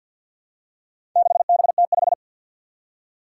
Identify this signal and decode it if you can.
Morse code, heard as a keyed tone, about 52 wpm, 700 Hz. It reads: 6BT5
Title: Four Challenging Letter-Number Combinations 50wpm